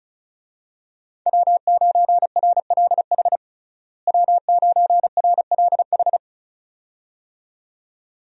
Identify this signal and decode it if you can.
Morse code, heard as a keyed tone, about 36 wpm, 700 Hz. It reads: W9RLH W9RLH
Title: USA Call Signs - 5 Characters - Repeated 2x - 35wpm